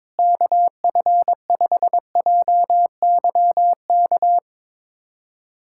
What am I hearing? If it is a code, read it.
Morse code, heard as a keyed tone, about 21 wpm, 700 Hz. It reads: KF5JYK
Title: USA Callsign DE Callsign 22wpm